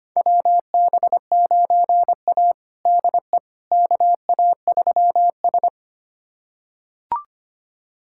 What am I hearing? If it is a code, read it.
Morse code, heard as a keyed tone, about 25 wpm, 700 Hz. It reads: WB9A DE KA3S
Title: USA Callsign DE Callsign - 4 Characters - 25wpm